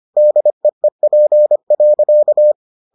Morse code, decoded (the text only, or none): DEEP.